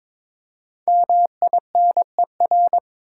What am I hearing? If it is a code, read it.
Morse code, heard as a keyed tone, about 22 wpm, 700 Hz. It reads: MINER
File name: 5 Letter Words - First 3 Letters Given - 22wpm